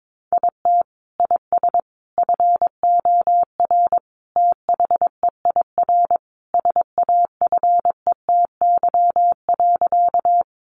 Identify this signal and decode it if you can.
Morse code, heard as a keyed tone, about 22 wpm, 700 Hz. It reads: IT IS FOR THEIR SAFETY.
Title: Sentences from Top 700 Words 22wpm